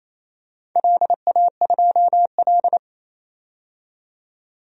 Morse code, heard as a keyed tone, about 27 wpm, 700 Hz. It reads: LA2L
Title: DXCC-100 Call Signs - 4 Characters - 28wpm